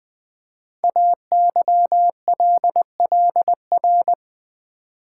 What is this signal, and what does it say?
Morse code, heard as a keyed tone, about 20 wpm, 700 Hz. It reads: AYLLR
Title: Code Groups - 5 Letters - 20wpm